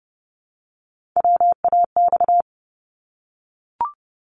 Morse code, heard as a keyed tone, about 30 wpm, 700 Hz. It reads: WAX
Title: Multi-letter Phonemes Repeated 3x 30wpm